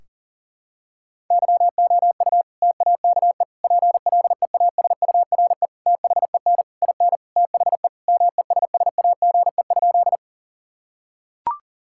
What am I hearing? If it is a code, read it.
Morse code, heard as a keyed tone, about 40 wpm, 700 Hz. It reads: YOU TAKE PLEASURE THEN IN THE MESSAGE?
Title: Sentences from Top 1600 Words 40wpm